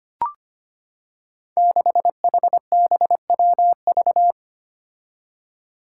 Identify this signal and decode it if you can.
Morse code, heard as a keyed tone, about 25 wpm, 700 Hz. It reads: 6HBWV
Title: Five Challenging Letter-Number Combinations 25wpm